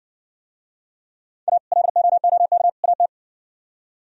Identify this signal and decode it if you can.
Morse code, heard as a keyed tone, about 67 wpm, 700 Hz. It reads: A LOOK AT